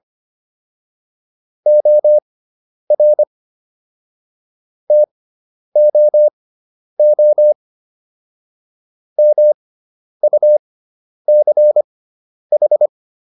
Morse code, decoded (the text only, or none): OR TOO MUCH